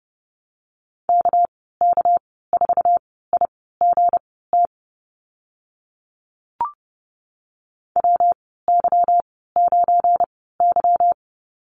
Morse code, heard as a keyed tone, about 30 wpm, spaced as for 17 wpm, 700 Hz. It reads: KK4IGT WY9Y
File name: Lesson 37g - Letter x - Call Signs 30wpm